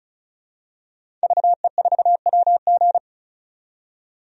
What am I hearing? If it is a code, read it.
Morse code, heard as a keyed tone, about 34 wpm, 700 Hz. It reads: VE4WG
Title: DXCC-100 Call Signs - 5 Characters - 35wpm